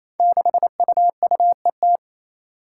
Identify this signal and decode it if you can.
Morse code, heard as a keyed tone, about 29 wpm, 700 Hz. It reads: 6UUET
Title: Five Challenging Letter-Number Combinations 28wpm